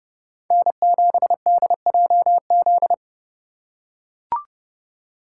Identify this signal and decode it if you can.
Morse code, heard as a keyed tone, about 29 wpm, 700 Hz. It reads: N7DJZ